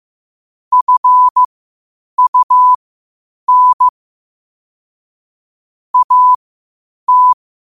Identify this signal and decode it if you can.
Morse code, heard as a keyed tone, about 15 wpm, spaced as for 8 wpm, 1000 Hz. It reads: FUN AT
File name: qotdfiles / 8wpm